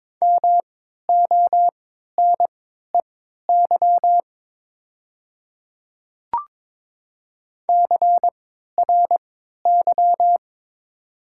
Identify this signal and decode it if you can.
Morse code, heard as a keyed tone, about 22 wpm, spaced as for 12 wpm, 690 Hz. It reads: MONEY CRY
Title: 3x Character Spacing - Top 500 Words - 22wpm